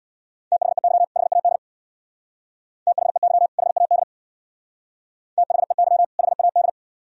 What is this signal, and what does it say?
Morse code, heard as a keyed tone, about 59 wpm, 700 Hz. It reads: THEY HAD THEY HAD THEY HAD
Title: Speed-Racing - Sets of 2 Words 50wpm